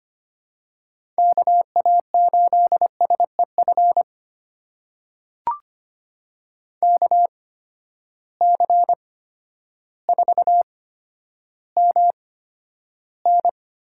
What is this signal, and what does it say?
Morse code, heard as a keyed tone, about 25 wpm, spaced as for 11 wpm, 700 Hz. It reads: KA8SEF KC4MN
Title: USA Call Signs - 8x Character Spacing - 25wpm